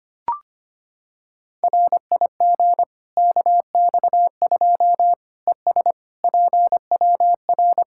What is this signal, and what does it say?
Morse code, heard as a keyed tone, about 25 wpm, 700 Hz. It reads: RIG KX2 ES PWR
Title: K1USN SST QSO Practice 25wpm